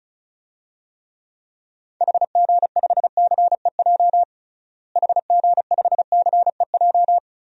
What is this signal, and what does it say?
Morse code, heard as a keyed tone, about 34 wpm, 700 Hz. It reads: HG5CEJ HG5CEJ
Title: DXCC-100 Call Signs - 6 Characters - Repeated 3x - 35wpm